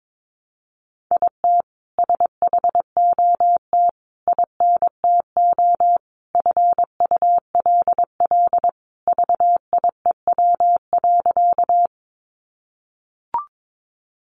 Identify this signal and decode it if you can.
Morse code, heard as a keyed tone, about 22 wpm, 700 Hz. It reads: IT SHOT INTO FULL VIEW.
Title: Sentences from Top 1100 Words 22wpm